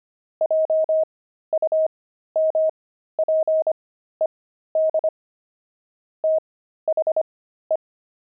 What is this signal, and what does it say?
Morse code, heard as a keyed tone, about 25 wpm, spaced as for 13 wpm, 630 Hz. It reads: JUMPED THE
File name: pangram-fs-13-cs-25-squirrel